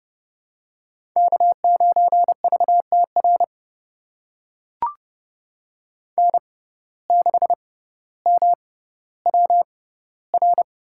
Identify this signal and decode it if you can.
Morse code, heard as a keyed tone, about 30 wpm, spaced as for 15 wpm, 700 Hz. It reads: K9VTR N6MWR
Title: USA Call Signs - 5 Characters - 6x Character Spacing - 30wpm